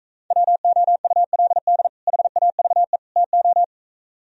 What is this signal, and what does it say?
Morse code, heard as a keyed tone, about 35 wpm, 700 Hz. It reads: WOULD HAVE TO
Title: Speed-Racing - Sets of 3 Words 28wpm